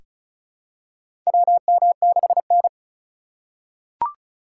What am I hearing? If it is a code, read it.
Morse code, heard as a keyed tone, about 34 wpm, 700 Hz. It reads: WM6N
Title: USA Call Signs - 4 Characters - 4x Character Spacing - 35wpm